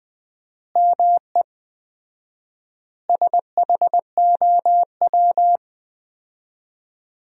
Morse code, heard as a keyed tone, about 20 wpm, 700 Hz. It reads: ME SHOW
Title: 4x Word-Spacing - Sentences from Top 500 Words - 20wpm